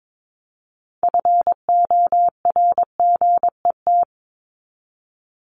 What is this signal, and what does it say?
Morse code, heard as a keyed tone, about 22 wpm, 700 Hz. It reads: FORGET
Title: Top 1001-1250 Words 22wpm